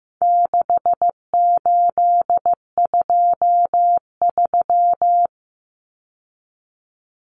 Morse code, heard as a keyed tone, about 15 wpm, 700 Hz. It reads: 6823
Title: Numbers - 4-Digit 15wpm